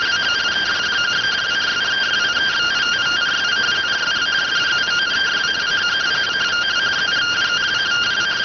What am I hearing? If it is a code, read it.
Two-tone teleprinter signal, about 170 Hz shift.